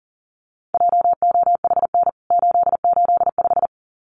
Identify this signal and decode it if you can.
Morse code, heard as a keyed tone, about 40 wpm, 700 Hz. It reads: JOHN 885
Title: CWT Exchange 40wpm